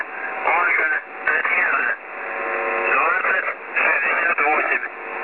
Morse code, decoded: none